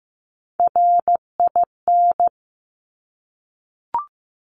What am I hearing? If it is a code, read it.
Morse code, heard as a keyed tone, about 15 wpm, 700 Hz. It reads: RIN
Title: Top Three Letter Combinations in Words 15wpm